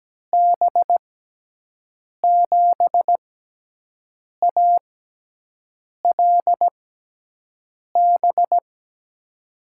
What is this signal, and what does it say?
Morse code, heard as a keyed tone, about 17 wpm, spaced as for 6 wpm, 700 Hz. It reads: B7ALB
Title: USA Call Signs - 6x Character Spacing - 17wpm